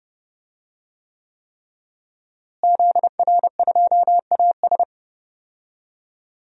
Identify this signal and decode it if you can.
Morse code, heard as a keyed tone, about 30 wpm, 700 Hz. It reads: ZR2AS
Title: DXCC-100 Call Signs - 5 Characters - Repeated 3x - 30wpm